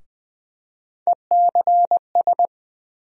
Morse code, heard as a keyed tone, about 20 wpm, 700 Hz. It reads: ECS
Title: USA Call Sign Postfixes - 3 Letters 20wpm